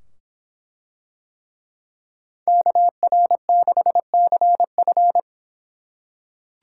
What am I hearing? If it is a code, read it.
Morse code, heard as a keyed tone, about 24 wpm, 700 Hz. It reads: KR6CF